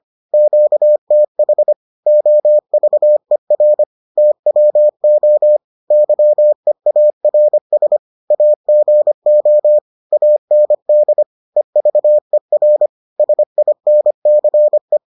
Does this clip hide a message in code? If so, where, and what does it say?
Yes: Morse code from 0.3 to 15.0 s, heard as a keyed tone, about 25 wpm, 600 Hz. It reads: QTH OVER TWO YEARS AGO AND EVER SINCE